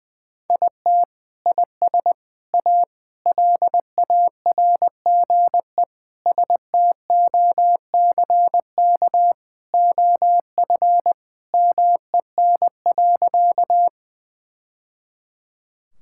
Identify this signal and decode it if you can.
Morse code, heard as a keyed tone, about 19 wpm, 700 Hz. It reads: IT IS A LARGE STOCK OF MEN.
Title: Sentences from Top 1000 Words - Review Part 5 - 20wpm